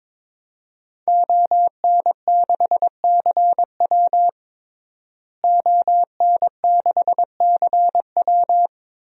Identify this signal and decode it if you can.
Morse code, heard as a keyed tone, about 21 wpm, 700 Hz. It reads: ON6CW ON6CW
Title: DXCC-100 Call Signs - 5 Characters - Repeated 3x - 22wpm